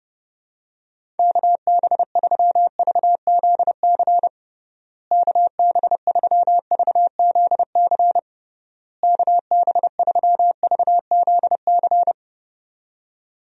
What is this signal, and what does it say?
Morse code, heard as a keyed tone, about 30 wpm, 700 Hz. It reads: KB3VZC KB3VZC KB3VZC
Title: USA Call Signs - 6 Characters - Repeated 3x - 30wpm